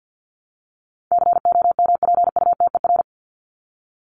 Morse code, heard as a keyed tone, about 50 wpm, 700 Hz. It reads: COMPUTER
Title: Top 501-750 Words 50wpm